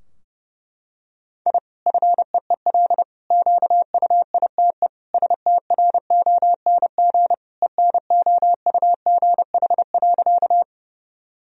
Morse code, heard as a keyed tone, about 30 wpm, 700 Hz. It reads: I FEEL QUITE STRONG ENOUGH.
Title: Sentences from Top 1000 Words - Review Part 3 - 30wpm